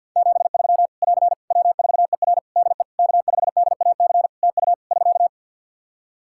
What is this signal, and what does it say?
Morse code, heard as a keyed tone, about 48 wpm, 700 Hz. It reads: 73 <AR> W4ER DE K5DAC TU <SK>
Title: K1USN SST QSO Practice 50wpm